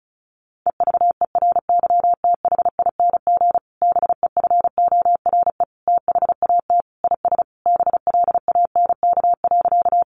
Morse code, heard as a keyed tone, about 34 wpm, 700 Hz. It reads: EVERYTHING BEFORE THAT IS BLANK.
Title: Sentences from Top 1500 Words 35wpm